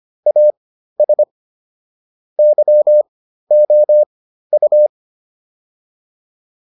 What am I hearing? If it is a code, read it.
Morse code, heard as a keyed tone, about 25 wpm, spaced as for 13 wpm, 600 Hz. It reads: AS YOU